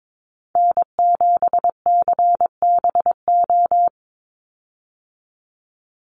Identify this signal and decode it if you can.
Morse code, heard as a keyed tone, about 22 wpm, 700 Hz. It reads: N7CBO
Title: USA Call Signs 22wpm